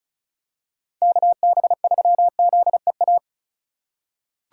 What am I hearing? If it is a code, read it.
Morse code, heard as a keyed tone, about 36 wpm, 700 Hz. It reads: KB3ZEA
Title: USA Call Signs - 6 Characters - 4x Character Spacing - 35wpm